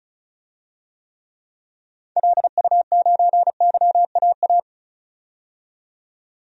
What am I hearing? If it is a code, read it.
Morse code, heard as a keyed tone, about 36 wpm, 700 Hz. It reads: LU9YAA
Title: DXCC-100 Call Signs - 6 Characters - 35wpm